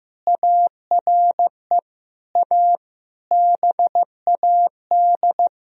Morse code, heard as a keyed tone, about 15 wpm, 700 Hz. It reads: ARE A BAD
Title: Sentences from Top 1000 Words - Review Part 3 - 15wpm